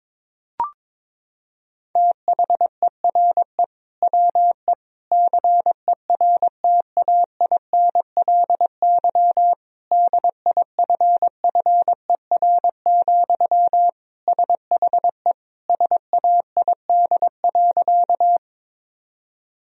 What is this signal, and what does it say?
Morse code, heard as a keyed tone, about 22 wpm, 700 Hz. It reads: THERE WE CERTAINLY DIFFER, SHE SAID.
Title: Sentences from Top 1000 Words - Review Part 1 - 22wpm